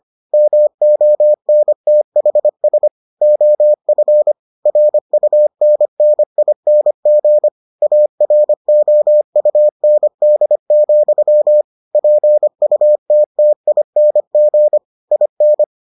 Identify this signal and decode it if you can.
Morse code, heard as a keyed tone, about 25 wpm, 600 Hz. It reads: MONTHS OF RUNNING AROUND, PUTTING IN